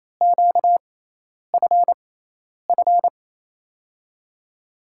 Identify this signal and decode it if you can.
Morse code, heard as a keyed tone, about 28 wpm, spaced as for 10 wpm, 700 Hz. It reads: QFF
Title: USA Call Signs - 6x Character Spacing - 28wpm